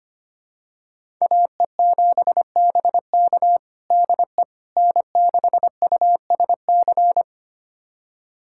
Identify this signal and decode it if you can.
Morse code, heard as a keyed tone, about 25 wpm, 700 Hz. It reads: AE7BK DE N6USC
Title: USA Callsign DE Callsign - 5 Characters - 25wpm